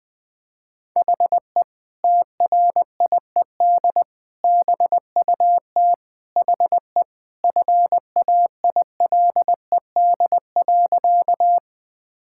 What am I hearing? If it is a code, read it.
Morse code, heard as a keyed tone, about 21 wpm, 700 Hz. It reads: HE TRIED BUT HE FAILED.
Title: Sentences with 2-Word Phrases - 20wpm